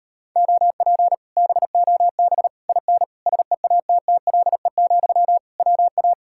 Morse code, heard as a keyed tone, about 38 wpm, 700 Hz. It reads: OP BOB IN SEATTLE, WA